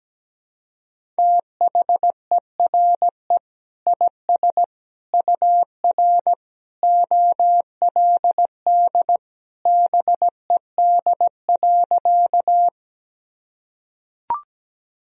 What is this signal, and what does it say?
Morse code, heard as a keyed tone, about 17 wpm, 700 Hz. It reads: THERE IS UR OLD BED.
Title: Sentences from Top 500 Words and Abbreviations 17wpm